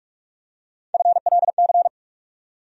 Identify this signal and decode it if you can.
Morse code, heard as a keyed tone, about 46 wpm, 700 Hz. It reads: FLC
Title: USA Call Sign Postfixes - 3 Letters 45wpm